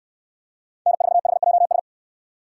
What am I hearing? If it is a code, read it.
Morse code, heard as a keyed tone, about 68 wpm, 700 Hz. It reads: N4HJS